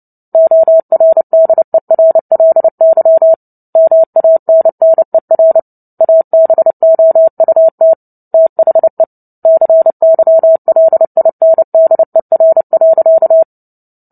Morse code, heard as a keyed tone, about 30 wpm, 650 Hz. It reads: ORDERLY MANNER ABOUT THE CYLINDER.
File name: Book Two / 03-Chapter 3 - 29 WPM